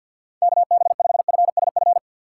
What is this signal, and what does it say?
Morse code, heard as a keyed tone, about 65 wpm, spaced as for 57 wpm, 700 Hz. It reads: KB5FSF
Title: Speed Racing - USA 2x3 Call Signs 50wpm